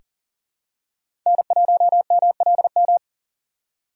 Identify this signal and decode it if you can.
Morse code, heard as a keyed tone, about 40 wpm, 700 Hz. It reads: N1MLM
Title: USA Call Signs - 6x Character Spacing - 40wpm